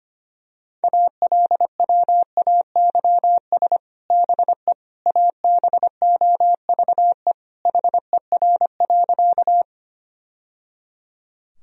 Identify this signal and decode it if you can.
Morse code, heard as a keyed tone, about 25 wpm, 700 Hz. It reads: ALWAYS BE ABOVE HER.
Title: Sentences from Top 1000 Words - Review Part 2 - 25wpm